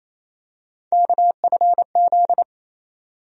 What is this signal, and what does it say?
Morse code, heard as a keyed tone, about 28 wpm, 700 Hz. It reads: KFZ
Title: USA Call Sign Postfixes - 3 Letters 28wpm